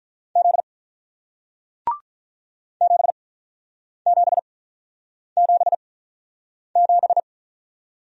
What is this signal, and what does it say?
Morse code, heard as a keyed tone, about 48 wpm, spaced as for 9 wpm, 700 Hz. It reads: Z 7777